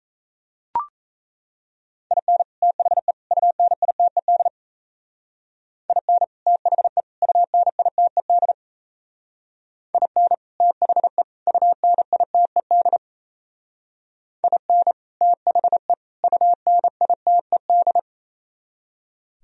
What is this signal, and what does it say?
Morse code, heard as a keyed tone, about 35 wpm, 700 Hz. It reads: IN THE UNITED IN THE UNITED IN THE UNITED IN THE UNITED